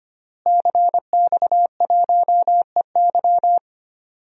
Morse code, heard as a keyed tone, about 25 wpm, 700 Hz. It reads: CX1EY
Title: DXCC-100 Call Signs - 5 Characters - Repeated 3x - 25wpm